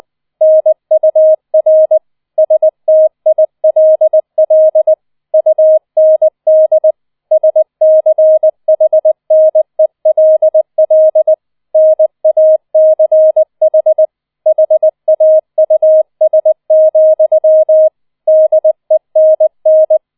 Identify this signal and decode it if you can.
Morse code, heard as a keyed tone, about 20 wpm, 630 Hz. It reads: NUR STILL UND SCHNELL NACH HAUS, DENN